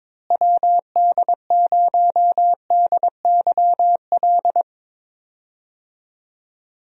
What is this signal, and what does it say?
Morse code, heard as a keyed tone, about 22 wpm, 700 Hz. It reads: WD0DYL
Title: USA Call Signs - 6 Characters - Repeated 2x - 22wpm